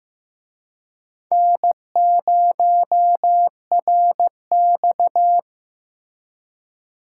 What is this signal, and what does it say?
Morse code, heard as a keyed tone, about 18 wpm, 700 Hz. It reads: N0RX
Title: Speed Racing - USA 1x2 Call Signs 15wpm